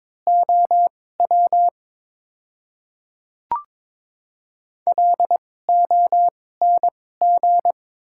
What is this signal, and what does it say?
Morse code, heard as a keyed tone, about 22 wpm, spaced as for 16 wpm, 700 Hz. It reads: OW LONG
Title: 2x Character Spacing - Top 200 Words - 22wpm